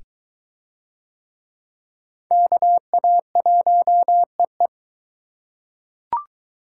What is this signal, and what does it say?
Morse code, heard as a keyed tone, about 21 wpm, 700 Hz. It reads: KA1EE